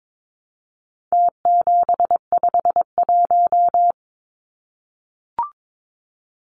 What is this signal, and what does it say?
Morse code, heard as a keyed tone, about 22 wpm, 700 Hz. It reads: T751
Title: Numbers - Contest Style - 4-Digit 22wpm